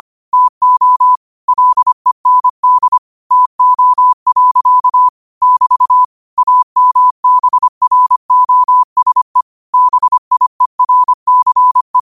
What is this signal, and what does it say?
Morse code, heard as a keyed tone, about 25 wpm, 1000 Hz. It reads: TO LEND TO. <BT> AMBROSE BIERCE
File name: qotdfiles / 25wpm